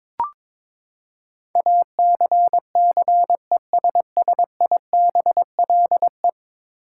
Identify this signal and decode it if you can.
Morse code, heard as a keyed tone, about 23 wpm, 700 Hz. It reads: ACCESSIBLE
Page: morse-code-ninja-practice-22wpm